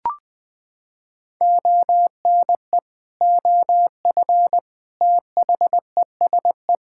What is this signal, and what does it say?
Morse code, heard as a keyed tone, about 20 wpm, 700 Hz. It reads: ONE OF THESE
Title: Sentences with 8-Word Phrases - 20wpm